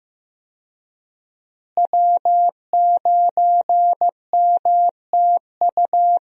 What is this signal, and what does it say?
Morse code, heard as a keyed tone, about 15 wpm, 700 Hz. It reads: W9MTU
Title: USA Call Signs - 5 Characters - Repeated 3x - 15wpm